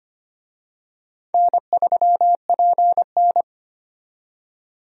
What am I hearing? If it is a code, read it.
Morse code, heard as a keyed tone, about 25 wpm, 700 Hz. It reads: N3PN
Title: USA Call Signs - 4 Characters - 25wpm